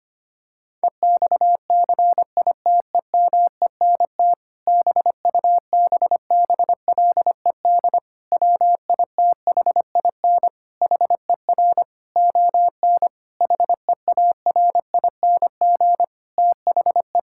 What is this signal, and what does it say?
Morse code, heard as a keyed tone, about 25 wpm, 700 Hz. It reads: EXCITEMENT BUBBLED WITHIN HER ON HEARING THE
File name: practice / 25wpm / Difficult Sentences with Hints - 25wpm